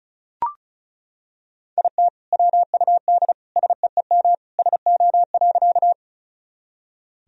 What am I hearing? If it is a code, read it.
Morse code, heard as a keyed tone, about 36 wpm, 700 Hz. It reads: IT WUD SEEM SO.
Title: Sentences from Top 300 Words and Abbreviations 35wpm